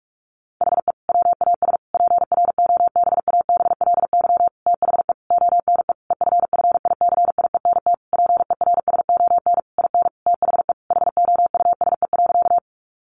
Morse code, heard as a keyed tone, about 44 wpm, 700 Hz. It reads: HE WAS PROBABLY THE ONE EFFICIENT PERSON IN THE HOUSE.
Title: Sentences from Top 1300 Words 45wpm